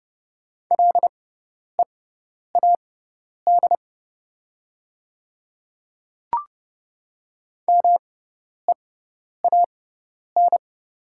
Morse code, heard as a keyed tone, about 30 wpm, spaced as for 10 wpm, 690 Hz. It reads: LEAD MEAN
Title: 6x Character Spacing - Top 500 Words - 30wpm